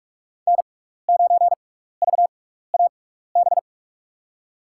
Morse code, heard as a keyed tone, about 44 wpm, spaced as for 15 wpm, 700 Hz. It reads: N9VAB
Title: USA Call Signs - 5 Characters - 6x Character Spacing - 45wpm